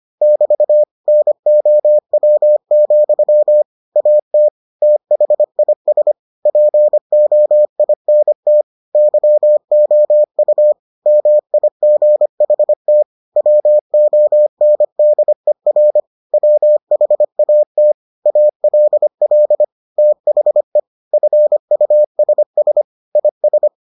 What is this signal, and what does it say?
Morse code, heard as a keyed tone, about 25 wpm, 600 Hz. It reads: <BT> NOW, AT THIS POINT YOU MIGHT WONDER WHAT ALL THE FUSS IS